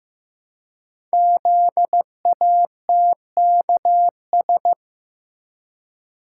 Morse code, heard as a keyed tone, about 15 wpm, 700 Hz. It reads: ZATKS
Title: Code Groups - 5 Letters - 15wpm